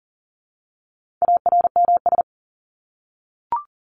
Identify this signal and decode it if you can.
Morse code, heard as a keyed tone, about 40 wpm, 700 Hz. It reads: ARMS
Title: ES:S Suffix - Top 100 Words with Base Words 40wpm